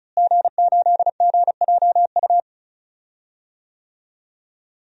Morse code, heard as a keyed tone, about 35 wpm, 700 Hz. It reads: G8GJU
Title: DXCC-100 Call Signs - 5 Characters - Repeated 2x - 35wpm